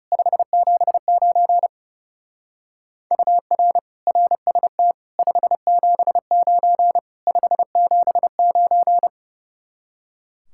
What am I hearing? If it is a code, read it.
Morse code, heard as a keyed tone, about 37 wpm, 700 Hz. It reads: 579 UR RST 579 579
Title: Speed-Racing Common QSO Phrases - 30wpm